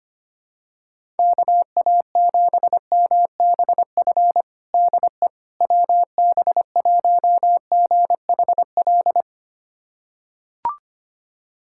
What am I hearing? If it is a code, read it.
Morse code, heard as a keyed tone, about 25 wpm, 700 Hz. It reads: KA7MBF DE WB1GHL
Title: USA Callsign DE Callsign - 6 Characters - 25wpm